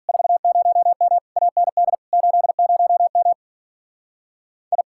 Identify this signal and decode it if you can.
Morse code, heard as a keyed tone, about 44 wpm, 700 Hz. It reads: 40M AND 80M I
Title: Speed-Racing Common QSO Phrases Encore - 35wpm